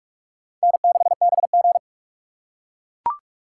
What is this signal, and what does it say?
Morse code, heard as a keyed tone, about 40 wpm, 700 Hz. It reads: N6BG